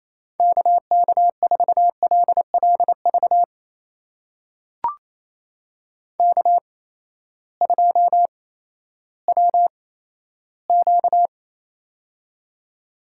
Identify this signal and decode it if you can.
Morse code, heard as a keyed tone, about 28 wpm, spaced as for 14 wpm, 700 Hz. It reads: KK4LLV K2WQ
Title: USA Call Signs - 8x Character Spacing - 28wpm